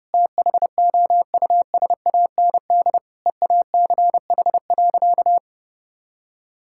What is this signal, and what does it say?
Morse code, heard as a keyed tone, about 30 wpm, 700 Hz. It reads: THOUSAND EACH.
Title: Sentences from Top 1000 Words - Review Part 5 - 30wpm